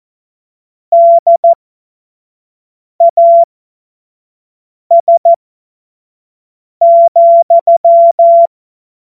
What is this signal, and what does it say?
Morse code, heard as a keyed tone, about 14 wpm, spaced as for 5 wpm, 680 Hz. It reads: DAS,
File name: Morsekurs / Witze / Witz 10. Mai 2022 25 F70